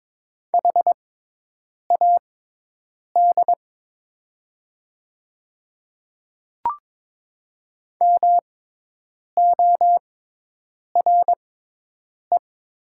Morse code, heard as a keyed tone, about 22 wpm, spaced as for 8 wpm, 700 Hz. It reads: HAD MORE
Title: 6x Character Spacing - Top 100 Words - 22wpm